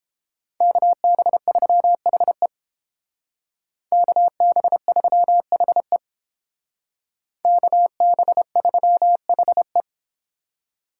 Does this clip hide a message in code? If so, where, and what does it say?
Yes: Morse code from 0.6 to 9.8 s, heard as a keyed tone, about 30 wpm, 700 Hz. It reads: KB3HE KB3HE KB3HE